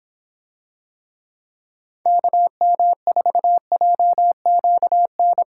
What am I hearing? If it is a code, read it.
Morse code, heard as a keyed tone, about 26 wpm, 700 Hz. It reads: KM4JQN